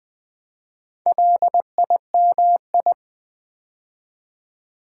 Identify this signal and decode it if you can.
Morse code, heard as a keyed tone, about 20 wpm, 700 Hz. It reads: LIMI
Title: Build A Word - 4 Letters At A Time - 10-Letter Words - 20wpm